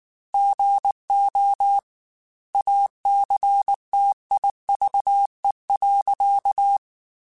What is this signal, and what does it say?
Morse code, heard as a keyed tone, about 19 wpm, 800 Hz. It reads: GO ACTIVE.
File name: Technical Subjects / Technical Subjects-mp3s / OR Gate @18